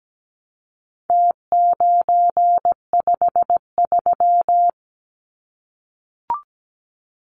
Morse code, heard as a keyed tone, about 17 wpm, 700 Hz. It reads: T953
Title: Numbers - Contest Style - 4-Digit 17wpm